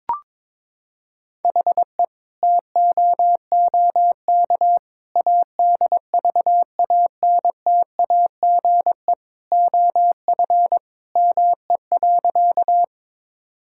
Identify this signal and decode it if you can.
Morse code, heard as a keyed tone, about 22 wpm, 700 Hz. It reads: HE TOOK ADVANTAGE OF ME.
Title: Sentences from Top 1000 Words - Review Part 6 - 22wpm